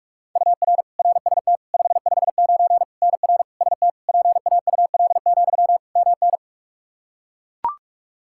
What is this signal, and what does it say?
Morse code, heard as a keyed tone, about 41 wpm, 700 Hz. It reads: UR RST 559 NR ST PAUL, MN
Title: Practice QSO - Exchange 1 - Beginning - 45wpm